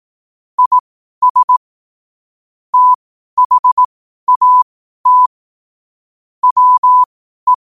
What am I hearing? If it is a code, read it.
Morse code, heard as a keyed tone, about 18 wpm, spaced as for 12 wpm, 1000 Hz. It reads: IS THAT WE
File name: qotdfiles / 12wpm